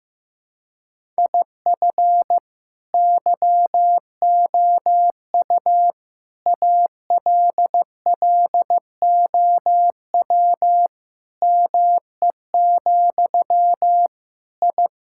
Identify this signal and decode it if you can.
Morse code, heard as a keyed tone, about 15 wpm, 700 Hz. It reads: IF YOU ALLOW ME, I
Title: Sentences from Top 1000 Words - Review Part 5 - 15wpm